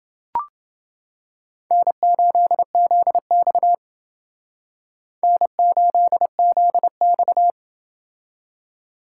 Morse code, heard as a keyed tone, about 28 wpm, 700 Hz. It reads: N8ZX N8ZX